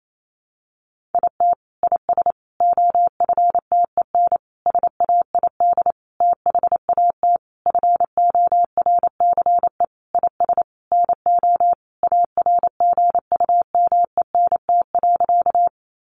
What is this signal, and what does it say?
Morse code, heard as a keyed tone, about 28 wpm, 700 Hz. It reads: IT IS OFTEN SAID THAT FORCE IS NO ARGUMENT.
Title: Sentences from Top 1400 Words 28wpm